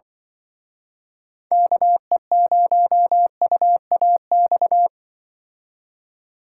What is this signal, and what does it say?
Morse code, heard as a keyed tone, about 23 wpm, 700 Hz. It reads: KE0UAX
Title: Speed Racing - USA 2x3 Call Signs 20wpm